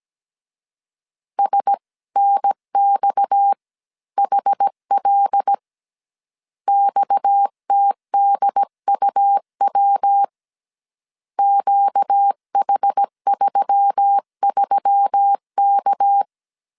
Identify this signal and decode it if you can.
Morse code, heard as a keyed tone, about 17 wpm, 800 Hz. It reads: SNXHL XTDUW QH33K
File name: CW / DET / 1 6 1500C S C 16wpm 2733